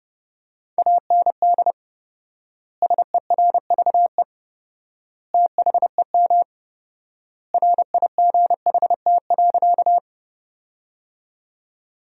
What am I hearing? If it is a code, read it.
Morse code, heard as a keyed tone, about 32 wpm, 700 Hz. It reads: AND SERVE THEM RIGHT.
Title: 4x Word-Spacing - Sentences from Top 400 Words - 30wpm